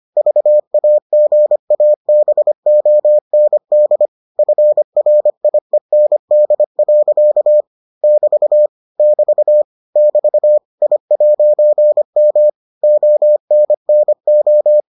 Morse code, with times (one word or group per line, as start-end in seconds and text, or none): 0.2-4.1 VAGABOND
4.4-7.6 FRIEND.
8.0-8.7 <BT>
9.0-9.6 <BT>
10.0-10.6 <BT>
10.8-12.5 I'M
12.8-14.8 ONNO